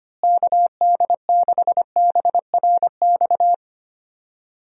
Morse code, heard as a keyed tone, about 25 wpm, 700 Hz. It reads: KD6BRX
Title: USA Call Signs - 6 Characters - Repeated 3x - 25wpm